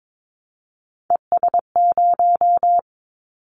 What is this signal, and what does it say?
Morse code, heard as a keyed tone, about 22 wpm, 700 Hz. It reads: ES0
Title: Callsign Prefixes for Top 100 DXCC - 22wpm